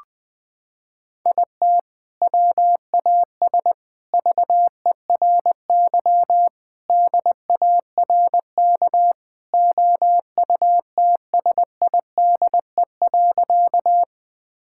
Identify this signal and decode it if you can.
Morse code, heard as a keyed tone, about 21 wpm, 700 Hz. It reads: IT WAS VERY DARK OUTSIDE.
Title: Sentences from Top 1000 Words - Review Part 6 - 20wpm